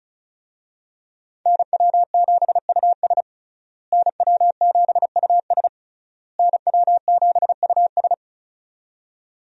Morse code, heard as a keyed tone, about 34 wpm, 700 Hz. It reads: NW7US NW7US NW7US
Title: USA Call Signs - 5 Characters - Repeated 3x - 35wpm